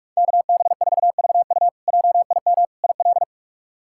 Morse code, heard as a keyed tone, about 42 wpm, 700 Hz. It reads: KB4VU JIM IL
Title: K1USN SST QSO Practice - 45wpm